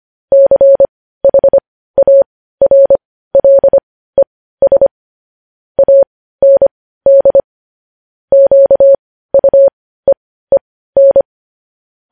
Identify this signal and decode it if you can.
Morse code, heard as a keyed tone, about 25 wpm, spaced as for 15 wpm, 550 Hz. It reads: CHARLES AND QUEEN
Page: news-headlines-in-morse-code-at-15-wpm